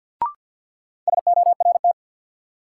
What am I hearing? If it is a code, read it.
Morse code, heard as a keyed tone, about 47 wpm, 700 Hz. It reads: SORT
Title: Sentences from Top 1000 Words - Review Part 3 - New Word Drill - 50wpm